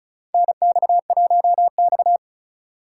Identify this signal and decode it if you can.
Morse code, heard as a keyed tone, about 36 wpm, 700 Hz. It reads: NX1X
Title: USA Call Signs - 4x Character Spacing - 35wpm